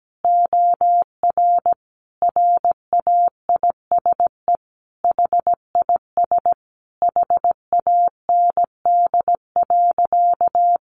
Morse code, heard as a keyed tone, about 17 wpm, 700 Hz. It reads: OR RAISE HIS HAND.
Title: Sentences from Top 1400 Words 17wpm